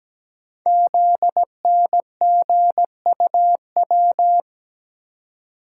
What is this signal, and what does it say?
Morse code, heard as a keyed tone, about 17 wpm, 700 Hz. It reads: ZNGUW
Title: Code Groups - 5 Letters - 17wpm